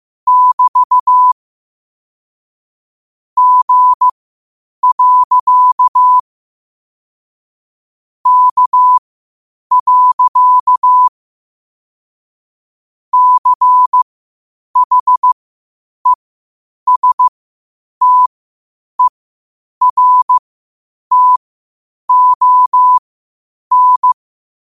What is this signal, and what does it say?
Morse code, heard as a keyed tone, about 15 wpm, spaced as for 8 wpm, 1000 Hz. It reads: <BT> G. K. CHESTERTON
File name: qotdfiles / 8wpm